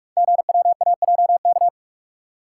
Morse code, heard as a keyed tone, about 44 wpm, 700 Hz. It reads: GWAJK
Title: Code Groups - 5 Letter:Numbers - 45wpm